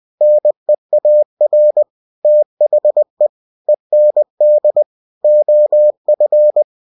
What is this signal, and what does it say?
Morse code, heard as a keyed tone, about 20 wpm, 600 Hz. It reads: NEAR THE END OF
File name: cw-books / winnie-the-pooh / winnie-20wpm-page-0111